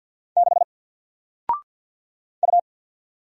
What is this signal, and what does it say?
Morse code, heard as a keyed tone, about 48 wpm, 700 Hz. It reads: 6 U
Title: Single Letter-Number Alt 50wpm